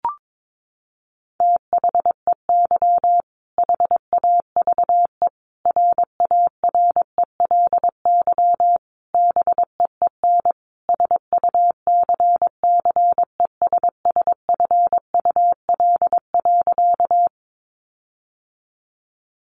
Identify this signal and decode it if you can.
Morse code, heard as a keyed tone, about 22 wpm, 700 Hz. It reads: THEY HAVE RARELY BEEN SUCCESSFUL.
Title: Sentences from Top 1400 Words 22wpm